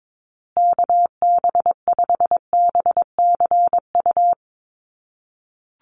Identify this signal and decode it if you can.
Morse code, heard as a keyed tone, about 22 wpm, 700 Hz. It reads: KB5BCU
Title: USA Call Signs Repeated 2x 22wpm